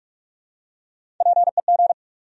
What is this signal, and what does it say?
Morse code, heard as a keyed tone, about 46 wpm, 700 Hz. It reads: PEG
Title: USA Common Names - Max 3 Letters - 45wpm